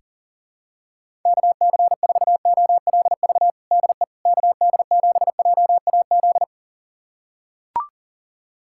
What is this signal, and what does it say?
Morse code, heard as a keyed tone, about 40 wpm, 700 Hz. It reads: KC4OLV DE KD7JAZ
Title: USA Callsign DE Callsign - 6 Characters - 40wpm